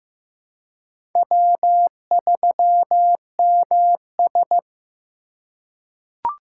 Morse code, heard as a keyed tone, about 15 wpm, 700 Hz. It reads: W3MS
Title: K1USN SST QSO Practice - 15wpm